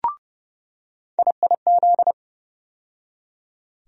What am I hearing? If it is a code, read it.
Morse code, heard as a keyed tone, about 31 wpm, 700 Hz. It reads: IIZ
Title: USA Call Sign Postfixes - 3 Letters 30wpm